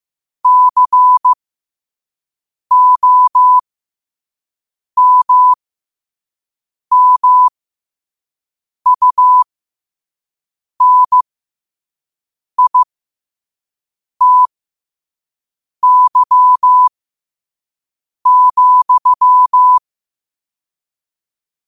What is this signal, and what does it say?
Morse code, heard as a keyed tone, about 15 wpm, spaced as for 5 wpm, 1000 Hz. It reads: COMMUNITY,
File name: qotdfiles / 5wpm